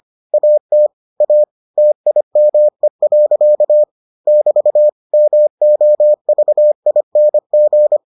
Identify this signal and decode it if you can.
Morse code, heard as a keyed tone, about 25 wpm, 600 Hz. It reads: AT A TIME. <BT> MOVING